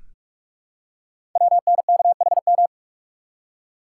Morse code, heard as a keyed tone, about 44 wpm, 700 Hz. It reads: WNKHM
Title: Code Groups - 5 Letters - 45wpm